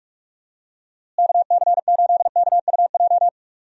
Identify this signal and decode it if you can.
Morse code, heard as a keyed tone, about 45 wpm, 700 Hz. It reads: KC8KUJ
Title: USA Call Signs - 6 Characters - Repeated 3x - 45wpm